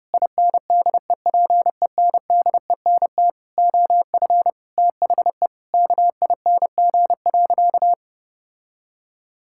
Morse code, heard as a keyed tone, about 30 wpm, 700 Hz. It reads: INDEPENDENT OF THE KING.
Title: Sentences from Top 1000 Words - Review Part 6 - 30wpm